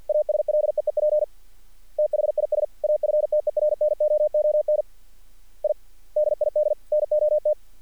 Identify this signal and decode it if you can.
Morse code, heard as a keyed tone, about 46 wpm, 600 Hz. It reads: ASLEEP THIS AFTERNOON I DID NOT
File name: archived / CRSnet